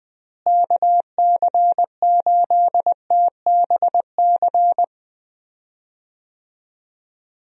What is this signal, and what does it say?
Morse code, heard as a keyed tone, about 20 wpm, 700 Hz. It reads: KC8TBC
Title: USA Call Signs - 6 Characters - Repeated 2x - 20wpm